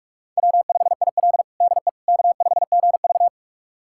Morse code, heard as a keyed tone, about 46 wpm, 700 Hz. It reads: W5IL DE K5GV
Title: USA Callsign DE Callsign - 4 Characters - 45wpm